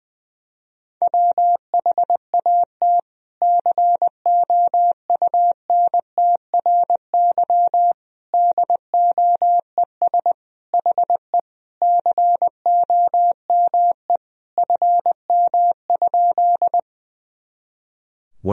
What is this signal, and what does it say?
Morse code, heard as a keyed tone, about 19 wpm, 700 Hz. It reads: WHAT COUNTRY DOES HE COME FM?
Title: Sentences from Top 300 Words and Abbreviations 20wpm